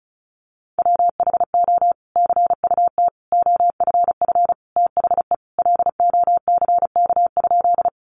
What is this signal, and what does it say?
Morse code, heard as a keyed tone, about 34 wpm, 700 Hz. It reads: WHO CUT OFF THE LOCK?
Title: Sentences from Top 1300 Words 35wpm